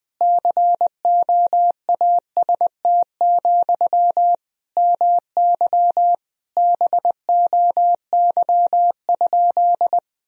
Morse code, heard as a keyed tone, about 20 wpm, 700 Hz. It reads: COAST, MY BOY?
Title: Sentences from Top 1000 Words - Review Part 4 - 20wpm